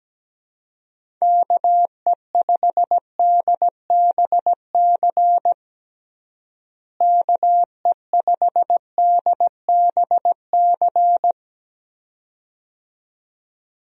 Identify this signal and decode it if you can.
Morse code, heard as a keyed tone, about 17 wpm, 700 Hz. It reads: KE5DBC KE5DBC
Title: USA Call Signs - 6 Characters - Repeated 2x - 17wpm